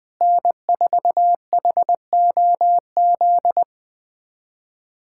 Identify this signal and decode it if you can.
Morse code, heard as a keyed tone, about 21 wpm, 700 Hz. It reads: N4HOZ
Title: USA Callsign DE Callsign - 5 Characters - 20wpm